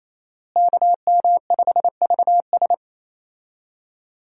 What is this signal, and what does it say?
Morse code, heard as a keyed tone, about 28 wpm, 700 Hz. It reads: KM5VS
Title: USA Call Signs - 5 Characters - Repeated 3x - 28wpm